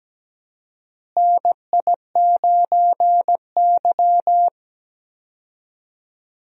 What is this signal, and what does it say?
Morse code, heard as a keyed tone, about 17 wpm, 700 Hz. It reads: NI9Y
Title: USA Call Signs - 4 Characters - Repeated 3x - 17wpm